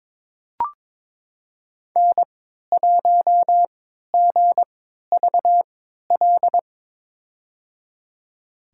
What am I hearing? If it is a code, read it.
Morse code, heard as a keyed tone, about 22 wpm, spaced as for 13 wpm, 700 Hz. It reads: N1GVL
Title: USA Call Signs - 3x Character Spacing - 22wpm